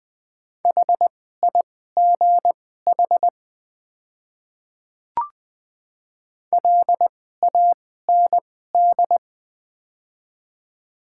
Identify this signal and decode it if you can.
Morse code, heard as a keyed tone, about 20 wpm, spaced as for 15 wpm, 700 Hz. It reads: HIGH LAND
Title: 2x Character Spacing - Top 500 Words - 20wpm